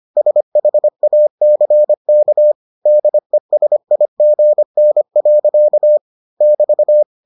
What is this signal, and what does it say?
Morse code, heard as a keyed tone, about 25 wpm, 600 Hz. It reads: SHACK DESIGN. <BT>